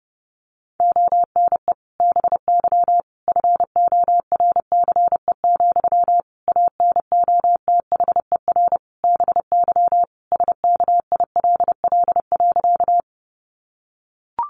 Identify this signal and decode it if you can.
Morse code, heard as a keyed tone, about 28 wpm, 700 Hz. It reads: ONE BY FORCE, ANOTHER BY SKILL.
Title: Sentences from Top 1200 Words 30wpm